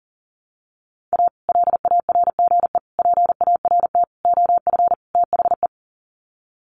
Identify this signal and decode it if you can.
Morse code, heard as a keyed tone, about 40 wpm, 700 Hz. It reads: A LARGE PART OF THE
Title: Sets of 5 Words Encore2 40wpm